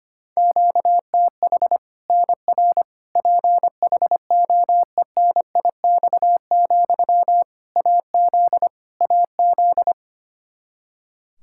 Morse code, heard as a keyed tone, about 25 wpm, 700 Hz. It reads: QTH NR PHOENIX, AZ AZ
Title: K1USN SST QSO Practice 25wpm